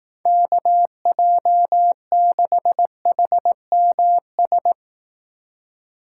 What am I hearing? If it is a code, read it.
Morse code, heard as a keyed tone, about 18 wpm, 700 Hz. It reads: KJ6HMS
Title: Speed Racing - USA 2x3 Call Signs 15wpm